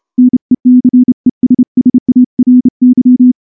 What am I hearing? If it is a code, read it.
Morse code, heard as a keyed tone, about 28 wpm, 250 Hz. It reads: NECESSARY
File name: morse-training-30WPM